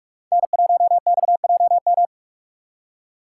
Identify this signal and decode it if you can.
Morse code, heard as a keyed tone, about 46 wpm, 700 Hz. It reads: N1XJM
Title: K1USN SST QSO Practice - 45wpm